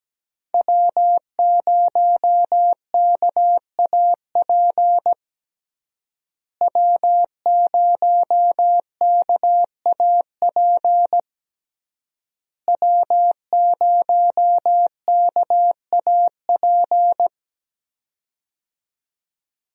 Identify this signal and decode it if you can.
Morse code, heard as a keyed tone, about 17 wpm, 700 Hz. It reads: W0KAP W0KAP W0KAP
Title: USA Call Signs - 5 Characters - Repeated 3x - 17wpm